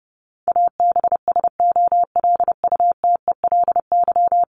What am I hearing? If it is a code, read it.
Morse code, heard as a keyed tone, about 30 wpm, 700 Hz. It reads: ABSOLUTELY
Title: LY Suffix - Top 100 Words with Base Words 30wpm